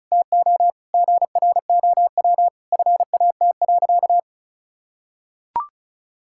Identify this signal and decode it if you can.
Morse code, heard as a keyed tone, about 36 wpm, 700 Hz. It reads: TO GROW FAT.
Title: Sentences from Top 1000 Words - Review Part 5 - 35wpm